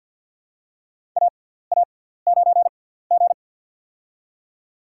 Morse code, heard as a keyed tone, about 52 wpm, spaced as for 17 wpm, 700 Hz. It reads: AA9G
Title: USA Call Signs - 4 Characters - 6x Character Spacing - 50wpm